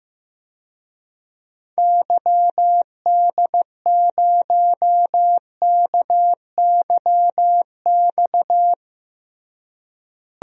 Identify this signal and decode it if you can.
Morse code, heard as a keyed tone, about 15 wpm, 700 Hz. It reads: YD0KYX
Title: DXCC-100 Call Signs - 6 Characters - Repeated 2x - 15wpm